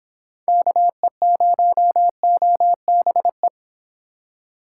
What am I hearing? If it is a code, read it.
Morse code, heard as a keyed tone, about 26 wpm, 700 Hz. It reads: KE0OBE
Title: Speed Racing - USA Call Signs 22wpm